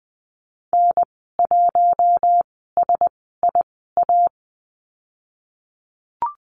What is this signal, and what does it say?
Morse code, heard as a keyed tone, about 20 wpm, spaced as for 14 wpm, 700 Hz. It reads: N1SIA
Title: 069 Lesson 12g - Letter d - Call Signs 20wpm